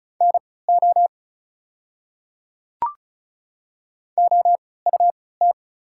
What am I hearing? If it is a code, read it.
Morse code, heard as a keyed tone, about 36 wpm, spaced as for 20 wpm, 690 Hz. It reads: NO OUT
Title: 3x Character Spacing - Top 200 Words - 35wpm